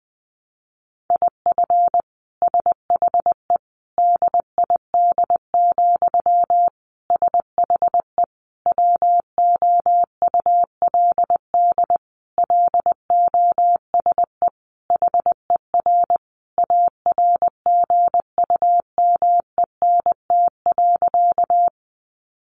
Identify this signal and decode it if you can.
Morse code, heard as a keyed tone, about 19 wpm, 700 Hz. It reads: IF SHE DID, SHE WOULD LOSE HER ARGUMENT.
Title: Sentences from Top 1400 Words 20wpm